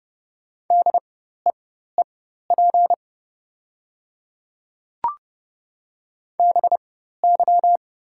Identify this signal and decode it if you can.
Morse code, heard as a keyed tone, about 30 wpm, spaced as for 14 wpm, 700 Hz. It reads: DEEP BY
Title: 4x Character Spacing - Top 500 Words - 30wpm